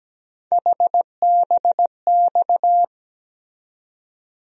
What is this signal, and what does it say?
Morse code, heard as a keyed tone, about 17 wpm, 700 Hz. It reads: HBX
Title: USA Call Sign Postfixes - 3 Letters 17wpm